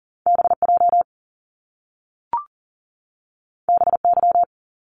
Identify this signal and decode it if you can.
Morse code, heard as a keyed tone, about 40 wpm, 700 Hz. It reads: BJ BY
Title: Top Two Letter Combinations in Words 40wpm